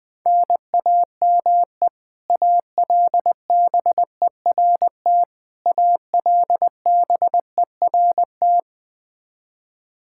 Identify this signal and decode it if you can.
Morse code, heard as a keyed tone, about 20 wpm, 700 Hz. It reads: NAME ALBERT ALBERT
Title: Practice QSO - Exchange 1 - Beginning - 20wpm